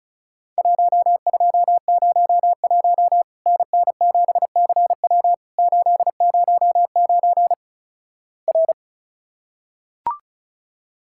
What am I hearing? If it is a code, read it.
Morse code, heard as a keyed tone, about 36 wpm, 700 Hz. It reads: 1201 NN7CW 809 R
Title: Worked All Europe DX Contest - QTC Exchange - 35wpm